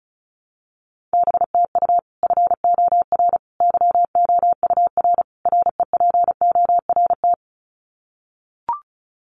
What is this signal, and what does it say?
Morse code, heard as a keyed tone, about 34 wpm, 700 Hz. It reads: BTU FOR YOUR REPORT
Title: Common QSO Phrases 35wpm